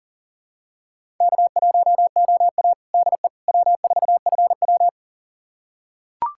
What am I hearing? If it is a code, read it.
Morse code, heard as a keyed tone, about 37 wpm, 700 Hz. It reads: K1OA DE W4FW
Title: USA Callsign DE Callsign - 4 Characters - 40wpm